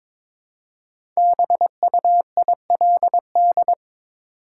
Morse code, heard as a keyed tone, about 22 wpm, 700 Hz. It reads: BUILD
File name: 5 Letter Words - First 3 Letters Given - 22wpm